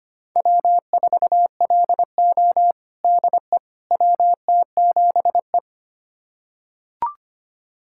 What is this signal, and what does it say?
Morse code, heard as a keyed tone, about 25 wpm, 700 Hz. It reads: W4LO DE WT7E
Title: USA Callsign DE Callsign - 4 Characters - 25wpm